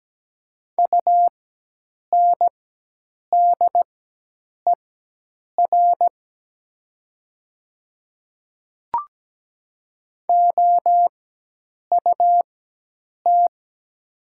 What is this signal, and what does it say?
Morse code, heard as a keyed tone, about 17 wpm, spaced as for 8 wpm, 700 Hz. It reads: UNDER OUT
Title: 4x Character Spacing - Top 200 Words - 17wpm